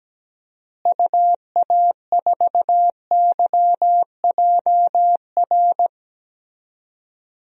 Morse code, heard as a keyed tone, about 17 wpm, 700 Hz. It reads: UA4YJR
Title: DXCC-100 Call Signs - 6 Characters - Repeated 3x - 17wpm